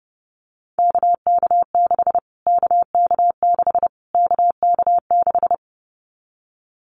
Morse code, heard as a keyed tone, about 28 wpm, 690 Hz. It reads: KK6 KK6 KK6
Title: Callsign Prefixes for Top 25 DXCC - Repeated 3x - 30wpm